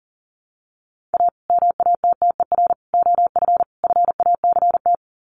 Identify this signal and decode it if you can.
Morse code, heard as a keyed tone, about 40 wpm, 700 Hz. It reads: A MATTER OF FACT
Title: Sets of 4 Words 40wpm